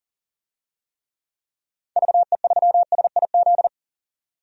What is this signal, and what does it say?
Morse code, heard as a keyed tone, about 40 wpm, 700 Hz. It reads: VE3SIZ
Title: DXCC-100 Call Signs - 6 Characters - 40wpm